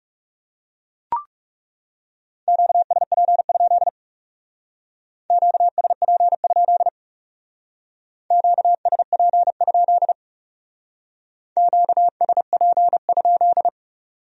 Morse code, heard as a keyed tone, about 38 wpm, 700 Hz. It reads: QSP? QSP? QSP? QSP?